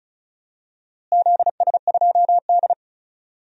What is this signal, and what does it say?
Morse code, heard as a keyed tone, about 36 wpm, 700 Hz. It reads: ZS2D
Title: DXCC-100 Call Signs - 4 Characters - Repeated 2x - 35wpm